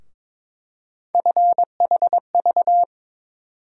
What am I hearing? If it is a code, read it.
Morse code, heard as a keyed tone, about 22 wpm, 700 Hz. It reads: FHV
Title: USA Call Sign Postfixes - 3 Letters 22wpm